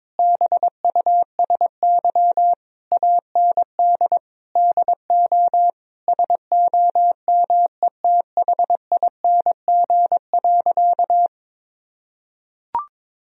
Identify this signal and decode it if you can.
Morse code, heard as a keyed tone, about 22 wpm, 700 Hz. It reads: BUSY AND DO SOMETHING.
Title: Sentences from Top 1000 Words - Review Part 3 - 22wpm